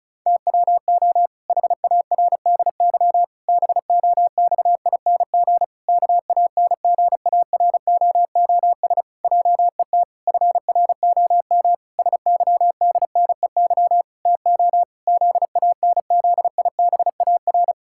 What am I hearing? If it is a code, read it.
Morse code, heard as a keyed tone, about 36 wpm, 700 Hz. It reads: TWO HARDY BOXING KANGAROOS JET FROM SYDNEY TO ZANZIBAR
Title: Pangram Challenge v2 - 35wpm